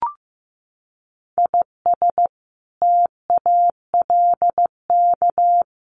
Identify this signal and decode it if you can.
Morse code, heard as a keyed tone, about 15 wpm, 700 Hz. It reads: IS TALK
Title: Sets of 5 Words 15wpm